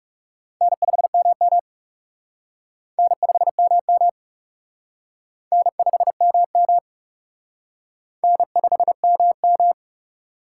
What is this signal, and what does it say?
Morse code, heard as a keyed tone, about 36 wpm, 700 Hz. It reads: N5MM N5MM N5MM N5MM
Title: Speed Racing - USA 1x2 Call Signs 30wpm